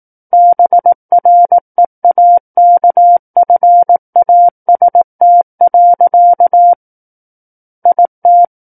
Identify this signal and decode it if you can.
Morse code, heard as a keyed tone, about 18 wpm, 700 Hz. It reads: BREAKFAST. IT
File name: 18 WPM